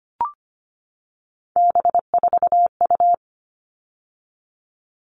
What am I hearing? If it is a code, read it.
Morse code, heard as a keyed tone, about 25 wpm, 700 Hz. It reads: B4U
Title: All Cut Numbers - 3-Digit 25wpm